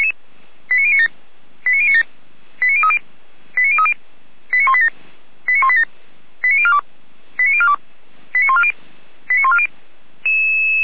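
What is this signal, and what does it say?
Five-tone selective call (ZVEI), heard as quick runs of five tones, tones of about 70 ms. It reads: 89097 89097 89030 89030 89187 89187 89042 89042 89140 89140
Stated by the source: Five-tone ZVEI selective call received by SDR.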